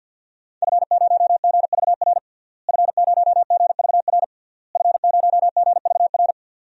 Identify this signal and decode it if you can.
Morse code, heard as a keyed tone, about 49 wpm, 700 Hz. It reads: F0GVR F0GVR F0GVR
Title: DXCC-100 Call Signs - 5 Characters - Repeated 3x - 50wpm